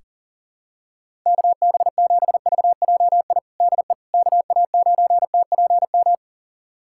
Morse code, heard as a keyed tone, about 40 wpm, 700 Hz. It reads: KB7VJI DE KA9TPM
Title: USA Callsign DE Callsign - 6 Characters - 40wpm